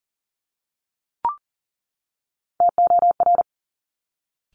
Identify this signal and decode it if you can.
Morse code, heard as a keyed tone, about 38 wpm, 700 Hz. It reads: TOR
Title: Top Three Letter Combinations in Words 40wpm